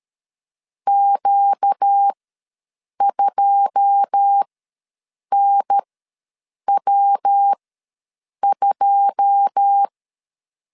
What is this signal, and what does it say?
Morse code, heard as a keyed tone, about 13 wpm, spaced as for 7 wpm, 800 Hz. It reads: Q2NW2
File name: LEZIONI / 9 2 ws 12wpm 3406